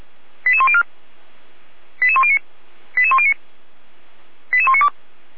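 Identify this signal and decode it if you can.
Five-tone selective call (ZVEI), heard as quick runs of five tones, tones of about 70 ms. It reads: 88104 88109 88109 88192
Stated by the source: ZVEI five-tone selective call 88103 received by SDR, from audio server of Feuerwehr Orlamünde (feuerwehr-orlamuende).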